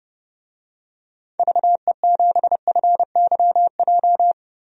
Morse code, heard as a keyed tone, about 30 wpm, 700 Hz. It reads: VE7FYJ
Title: DXCC-100 Call Signs - 6 Characters - 30wpm